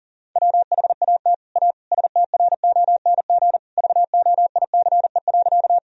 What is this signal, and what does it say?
Morse code, heard as a keyed tone, about 37 wpm, 700 Hz. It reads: WHAT A STRONG VOICE.
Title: Sentences from Top 1000 Words - Review Part 3 - 40wpm